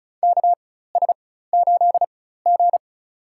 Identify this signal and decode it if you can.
Morse code, heard as a keyed tone, about 34 wpm, spaced as for 16 wpm, 700 Hz. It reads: KS8G
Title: USA Call Signs - 4 Characters - 4x Character Spacing - 35wpm